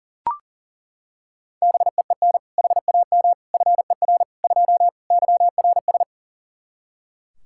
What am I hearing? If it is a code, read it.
Morse code, heard as a keyed tone, about 40 wpm, 700 Hz. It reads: BEEN HAM FER 2 YRS
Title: K1USN SST QSO Practice 40wpm